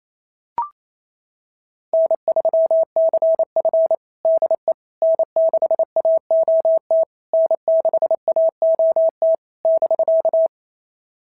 Morse code, heard as a keyed tone, about 27 wpm, 650 Hz. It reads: N3CF DE N6AOT N6AOT <BK>
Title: SOTA QSO Practice 28wpm